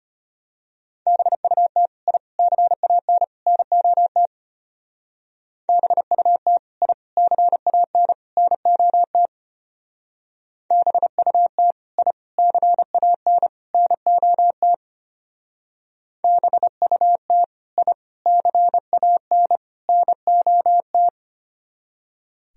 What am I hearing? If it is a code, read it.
Morse code, heard as a keyed tone, about 32 wpm, 700 Hz. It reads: BUT I CAN NOT BUT I CAN NOT BUT I CAN NOT BUT I CAN NOT